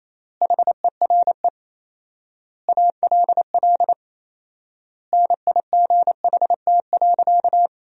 Morse code, heard as a keyed tone, about 28 wpm, 700 Hz. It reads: HERE ALL NIGHT.
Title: 4x Word-Spacing - Sentences from Top 500 Words - 28wpm